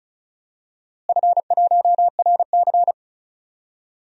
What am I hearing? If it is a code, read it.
Morse code, heard as a keyed tone, about 36 wpm, 700 Hz. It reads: F1RC
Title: DXCC-100 Call Signs - 4 Characters - Repeated 3x - 35wpm